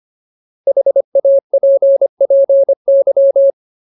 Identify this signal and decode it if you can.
Morse code, heard as a keyed tone, about 25 wpm, 550 Hz. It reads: HAPPY
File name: Ultimate Word List / 25WPM Ultimate